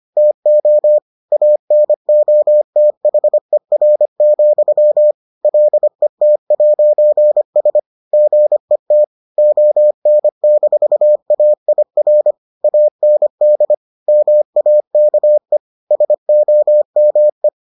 Morse code, heard as a keyed tone, about 25 wpm, 600 Hz. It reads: TO ANOTHER, LET'S GET ON-AIR AND MAKE SOME